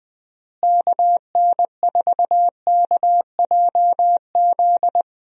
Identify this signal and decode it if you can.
Morse code, heard as a keyed tone, about 20 wpm, 700 Hz. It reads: KN4KJZ
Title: USA Call Signs - 6 Characters - Repeated 2x - 20wpm